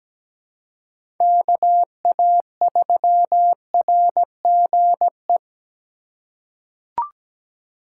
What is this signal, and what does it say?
Morse code, heard as a keyed tone, about 17 wpm, 700 Hz. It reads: KA3RGE
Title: USA Call Signs - 6 Characters - 2x Character Spacing - 17wpm